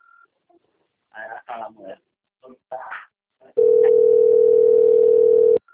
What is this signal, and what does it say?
Ringback tone (North America), 440 + 480 Hz, with speech.
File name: out- - - - - .69477